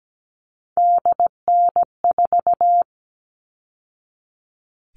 Morse code, heard as a keyed tone, about 17 wpm, 700 Hz. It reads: DN4
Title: All Cut Numbers - 3-Digit 17wpm